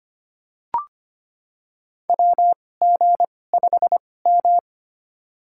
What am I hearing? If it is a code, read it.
Morse code, heard as a keyed tone, about 25 wpm, spaced as for 18 wpm, 700 Hz. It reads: WG5M
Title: USA Call Signs - 4 Characters - 2x Character Spacing - 25wpm